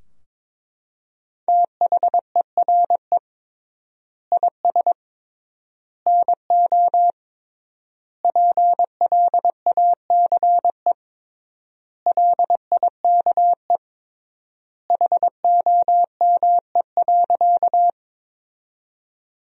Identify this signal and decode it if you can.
Morse code, heard as a keyed tone, about 22 wpm, 700 Hz. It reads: THERE IS NO PLACE LIKE HOME.
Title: 3x Word-Spacing - Sentences from Top 200 Words - 22wpm